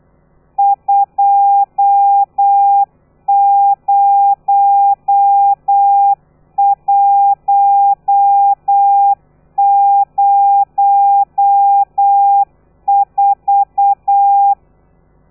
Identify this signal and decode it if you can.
Morse code, heard as a keyed tone, about 8 wpm, 800 Hz. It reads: 20104
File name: telegrafia / i2tfj / Lez13 35